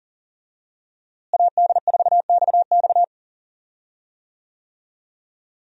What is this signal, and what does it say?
Morse code, heard as a keyed tone, about 40 wpm, 700 Hz. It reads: AD4XX
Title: USA Call Signs - 5 Characters - Repeated 3x - 40wpm